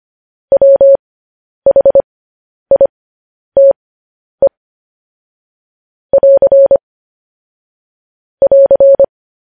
Morse code, heard as a keyed tone, about 25 wpm, spaced as for 10 wpm, 550 Hz. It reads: WHITE <AR> <AR>